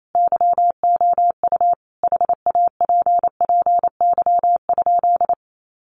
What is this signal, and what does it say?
Morse code, heard as a keyed tone, about 28 wpm, 700 Hz. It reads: YOU HAPPY?
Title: Sentences from Top 1000 Words 28wpm